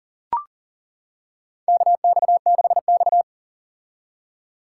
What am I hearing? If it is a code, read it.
Morse code, heard as a keyed tone, about 40 wpm, 690 Hz. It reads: KX6X
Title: K1USN SST QSO Practice - 40wpm